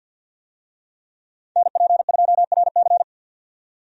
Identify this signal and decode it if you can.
Morse code, heard as a keyed tone, about 48 wpm, 700 Hz. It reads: NP2RC
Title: USA Call Signs - 5 Characters - 50wpm